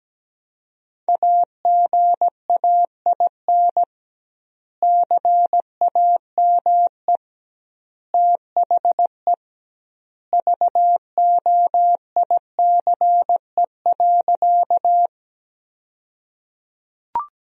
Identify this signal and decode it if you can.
Morse code, heard as a keyed tone, about 17 wpm, 700 Hz. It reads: AGAIN CAME THE VOICE.
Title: 2x Word-Spacing - Sentences from Top 500 Words - 17wpm